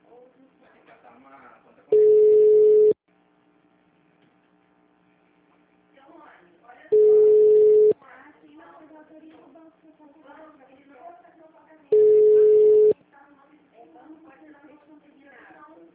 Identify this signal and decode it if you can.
Ringback tone (Europe, 425 Hz), with speech.